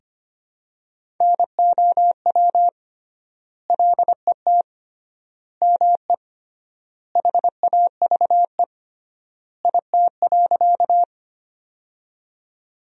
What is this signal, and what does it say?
Morse code, heard as a keyed tone, about 25 wpm, 700 Hz. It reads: NOW LET ME HAVE IT.
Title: 3x Word-Spacing - Sentences from Top 300 Words - 25wpm